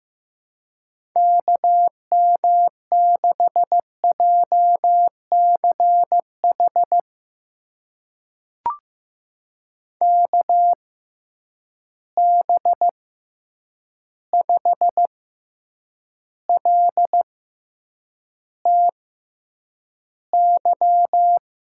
Morse code, heard as a keyed tone, about 16 wpm, spaced as for 8 wpm, 700 Hz. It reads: KM6JCH KB5LTY
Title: USA Call Signs - 6 Characters - 6x Character Spacing - 15wpm